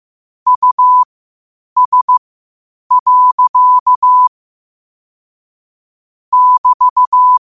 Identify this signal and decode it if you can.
Morse code, heard as a keyed tone, about 15 wpm, spaced as for 8 wpm, 1000 Hz. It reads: US. <BT>
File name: 8wpm